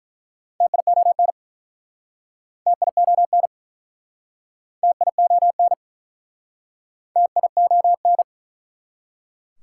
Morse code, heard as a keyed tone, about 49 wpm, 700 Hz. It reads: TION TION TION TION